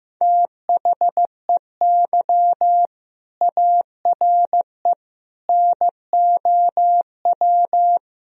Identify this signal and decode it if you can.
Morse code, heard as a keyed tone, about 15 wpm, 700 Hz. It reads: THEY ARE NOW
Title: Sentences from Top 1000 Words - Review Part 2 - 15wpm